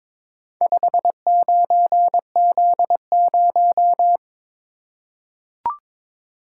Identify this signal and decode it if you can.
Morse code, heard as a keyed tone, about 23 wpm, 700 Hz. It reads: 59Z0
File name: practice / 22wpm / Four Letter-Number Combinations - 22wpm